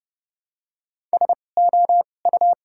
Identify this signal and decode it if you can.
Morse code, heard as a keyed tone, about 30 wpm, spaced as for 22 wpm, 700 Hz. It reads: SOU
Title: 2x Character Spacing - Top 100 Words - 30wpm